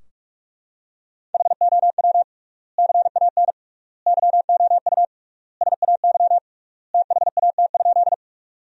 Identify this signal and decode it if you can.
Morse code, heard as a keyed tone, about 41 wpm, 700 Hz. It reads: HOW CAN YOU SAY THAT?
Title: 3x Word-Spacing - Sentences from Top 200 Words - 45wpm